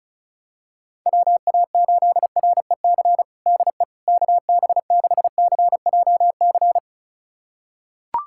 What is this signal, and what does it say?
Morse code, heard as a keyed tone, about 34 wpm, 700 Hz. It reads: WA8REC DE KB6CJC
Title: USA Callsign DE Callsign - 6 Characters - 35wpm